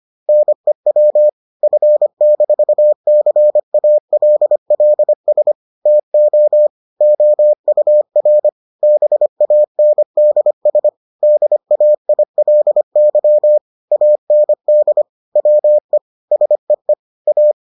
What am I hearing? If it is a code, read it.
Morse code, heard as a keyed tone, about 25 wpm, 600 Hz. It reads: NEW F-CALLS TO OUR BANDS DAILY AND WE SEE A